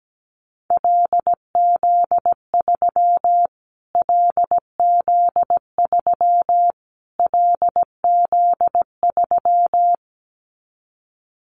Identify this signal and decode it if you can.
Morse code, heard as a keyed tone, about 17 wpm, 700 Hz. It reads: LZ3 LZ3 LZ3
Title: Callsign Prefixes for Top 25 DXCC - Repeated 3x - 17wpm